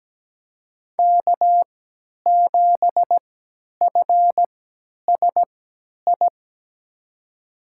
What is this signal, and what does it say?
Morse code, heard as a keyed tone, about 17 wpm, spaced as for 10 wpm, 700 Hz. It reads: K7FSI
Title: USA Call Signs - 5 Characters - 3x Character Spacing - 17wpm